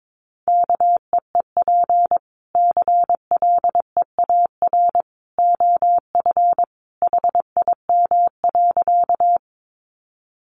Morse code, heard as a keyed tone, about 22 wpm, 700 Hz. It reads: KEEP CLEAR OF HIM.
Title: Sentences from Top 500 Words 22wpm